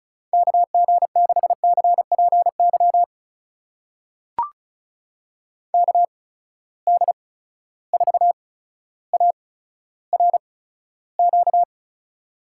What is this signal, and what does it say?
Morse code, heard as a keyed tone, about 34 wpm, spaced as for 15 wpm, 700 Hz. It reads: KG6CPY KD4ARQ
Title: USA Call Signs - 6 Characters - 8x Character Spacing - 35wpm